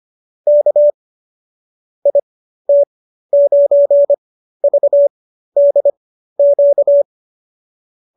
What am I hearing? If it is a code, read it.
Morse code, heard as a keyed tone, about 25 wpm, spaced as for 13 wpm, 590 Hz. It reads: K IT9VDQ